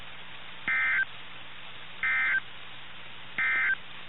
Emergency Alert System (SAME) tones, heard as harsh buzzing bursts. It is heard from 0.6 to 3.8 s.
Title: NOAA Weather Radio - KIH61